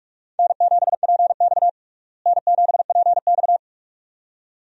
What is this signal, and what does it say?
Morse code, heard as a keyed tone, about 46 wpm, 700 Hz. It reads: N7PX N7PX
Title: USA Call Signs - 4 Characters - Repeated 2x - 45wpm